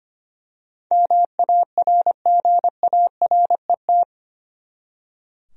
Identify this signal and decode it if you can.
Morse code, heard as a keyed tone, about 25 wpm, 700 Hz. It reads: MARGARET
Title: USA Common Names - 25wpm